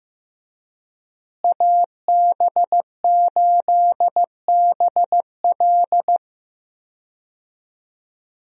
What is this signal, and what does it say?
Morse code, heard as a keyed tone, about 15 wpm, 700 Hz. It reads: AB8BL
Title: USA Call Signs - 5 Characters - Repeated 3x - 15wpm